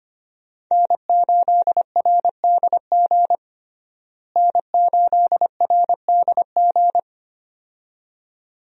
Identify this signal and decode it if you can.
Morse code, heard as a keyed tone, about 25 wpm, 700 Hz. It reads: N8RDG N8RDG
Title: USA Call Signs - 5 Characters - Repeated 2x - 25wpm